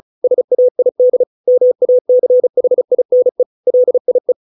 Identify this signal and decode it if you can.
Morse code, heard as a keyed tone, about 34 wpm, 500 Hz. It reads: SAID MACHINE LIE